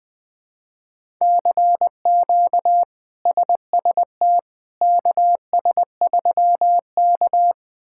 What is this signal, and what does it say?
Morse code, heard as a keyed tone, about 20 wpm, 700 Hz. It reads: CQ SST KS3K
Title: K1USN SST QSO Practice - 20wpm